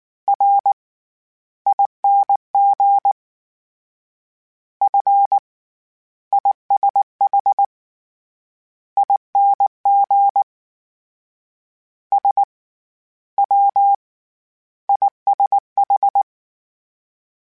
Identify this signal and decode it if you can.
Morse code, heard as a keyed tone, about 19 wpm, spaced as for 12 wpm, 800 Hz. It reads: RING FISHING SWISH